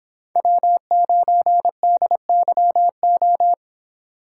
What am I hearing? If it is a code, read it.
Morse code, heard as a keyed tone, about 25 wpm, 700 Hz. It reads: W9DYO